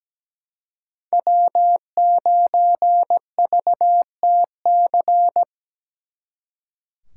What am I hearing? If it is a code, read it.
Morse code, heard as a keyed tone, about 20 wpm, 700 Hz. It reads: W9VTC
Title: Speed Racing - USA 1x3 Call Signs 17wpm